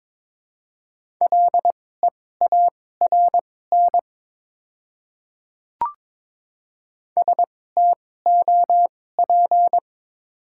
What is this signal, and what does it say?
Morse code, heard as a keyed tone, about 22 wpm, spaced as for 16 wpm, 700 Hz. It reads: LEARN STOP
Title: 2x Character Spacing - Top 500 Words - 22wpm